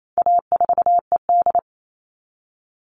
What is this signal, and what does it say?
Morse code, heard as a keyed tone, about 28 wpm, 700 Hz. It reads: A4ED
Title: All Cut Numbers - 4-Digit 28wpm